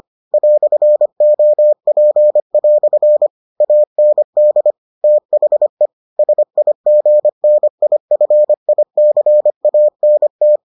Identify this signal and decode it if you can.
Morse code, heard as a keyed tone, about 25 wpm, 600 Hz. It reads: "OP" AND THE SIGNIFICANT